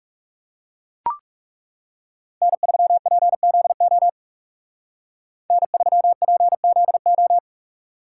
Morse code, heard as a keyed tone, about 37 wpm, 700 Hz. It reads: N3PZO N3PZO
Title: Speed Racing - USA Call Signs 30wpm